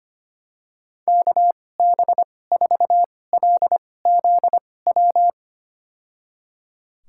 Morse code, heard as a keyed tone, about 25 wpm, spaced as for 18 wpm, 700 Hz. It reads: KB4LZW
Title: USA Call Signs - 2x Character Spacing - 25wpm